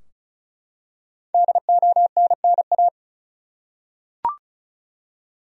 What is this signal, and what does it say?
Morse code, heard as a keyed tone, about 34 wpm, 700 Hz. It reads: DONNA
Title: USA Common Names - 35wpm